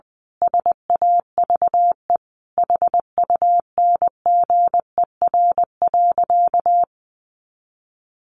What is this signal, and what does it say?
Morse code, heard as a keyed tone, about 20 wpm, 700 Hz. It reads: SAVE HUNGER.
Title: A Sound Of Thunder 20wpm